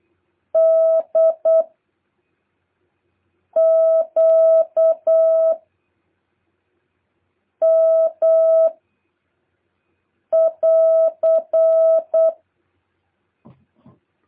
Morse code, heard as a keyed tone, about 8 wpm, 650 Hz. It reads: D Q M <AR>
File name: jh8cbh / 2023cw